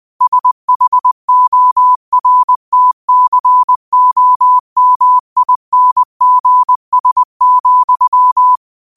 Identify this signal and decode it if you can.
Morse code, heard as a keyed tone, about 20 wpm, 1000 Hz. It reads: SHORTCOMINGS,
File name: qotdfiles / 20wpm